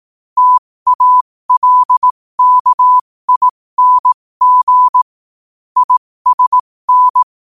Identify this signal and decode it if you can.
Morse code, heard as a keyed tone, about 18 wpm, spaced as for 15 wpm, 1000 Hz. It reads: TALKING ISN
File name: qotdfiles / 15wpm